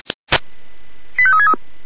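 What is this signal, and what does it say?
Five-tone selective call (ZVEI), heard as quick runs of five tones, tones of about 70 ms. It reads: 85363
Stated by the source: ZVEI five-tone selective call received on an SDR.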